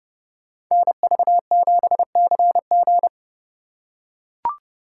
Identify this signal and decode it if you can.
Morse code, heard as a keyed tone, about 29 wpm, 700 Hz. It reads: NV7CG